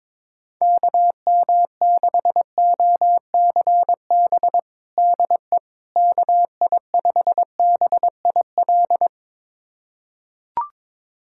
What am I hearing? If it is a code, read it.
Morse code, heard as a keyed tone, about 22 wpm, 700 Hz. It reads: KM6OCB DE KI5BIL
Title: USA Callsign DE Callsign - 6 Characters - 22wpm